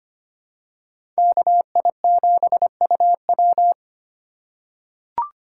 Morse code, heard as a keyed tone, about 25 wpm, 700 Hz. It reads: KI7UW
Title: USA Call Signs - 2x Character Spacing - 25wpm